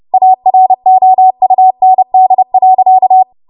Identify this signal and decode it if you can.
Morse code, heard as a keyed tone, about 30 wpm, 750 Hz. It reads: AROUND.
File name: Morse / 30WPM